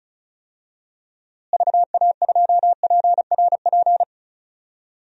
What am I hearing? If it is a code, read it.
Morse code, heard as a keyed tone, about 34 wpm, 700 Hz. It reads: VA2PRP
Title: DXCC-100 Call Signs - 6 Characters - Repeated 2x - 35wpm